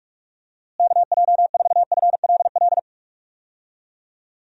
Morse code, heard as a keyed tone, about 46 wpm, 700 Hz. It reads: KJ4FLL
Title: USA Call Signs - 6 Characters - Repeated 3x - 45wpm